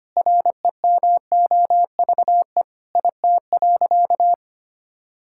Morse code, heard as a keyed tone, about 25 wpm, 700 Hz. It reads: REMOVE IT.
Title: Sentences from Top 1000 Words - Review Part 5 - 25wpm